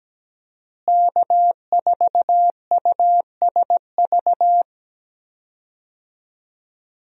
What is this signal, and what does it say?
Morse code, heard as a keyed tone, about 17 wpm, 700 Hz. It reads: K4USV
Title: USA Call Signs - 5 Characters - Repeated 2x - 17wpm